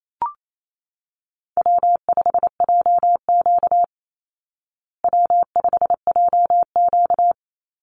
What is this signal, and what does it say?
Morse code, heard as a keyed tone, about 28 wpm, 700 Hz. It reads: W5JQ W5JQ
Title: Easy Call Signs Repeated 2x - 28wpm